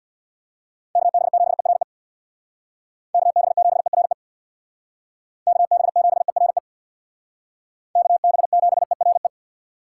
Morse code, heard as a keyed tone, about 59 wpm, 700 Hz. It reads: KB7ERE KB7ERE KB7ERE KB7ERE
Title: Speed Racing - USA 2x3 Call Signs 50wpm